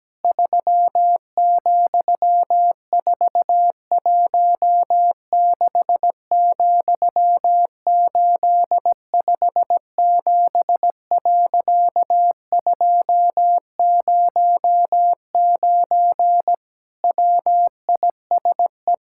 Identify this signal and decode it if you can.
Morse code, heard as a keyed tone, about 17 wpm, 700 Hz. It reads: 3,416,857.209 WISE
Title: Pangram Challenge v2 - 17wpm